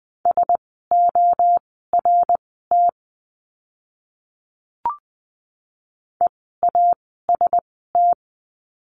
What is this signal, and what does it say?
Morse code, heard as a keyed tone, about 20 wpm, spaced as for 14 wpm, 700 Hz. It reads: SORT EAST
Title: 054 Lesson 10f - Letter r - Words 20wpm